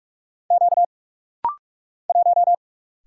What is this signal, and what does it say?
Morse code, heard as a keyed tone, about 46 wpm, 700 Hz. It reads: Q 1
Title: Single Letter-Number - Rapid-Fire 45wpm